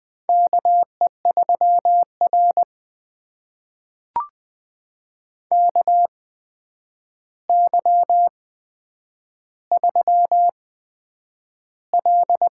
Morse code, heard as a keyed tone, about 20 wpm, spaced as for 9 wpm, 700 Hz. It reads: KE3R KY3L
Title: USA Call Signs - 4 Characters - 8x Character Spacing - 20wpm